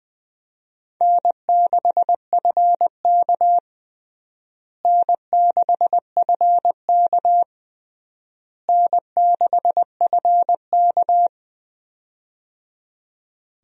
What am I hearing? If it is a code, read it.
Morse code, heard as a keyed tone, about 20 wpm, 700 Hz. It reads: N6FK N6FK N6FK
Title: USA Call Signs - 4 Characters - Repeated 3x - 20wpm